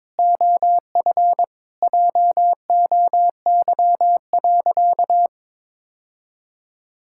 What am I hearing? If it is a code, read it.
Morse code, heard as a keyed tone, about 22 wpm, 700 Hz. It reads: OF JOY.
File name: practice / 22wpm / Sentences with 3-Word Phrases - 22wpm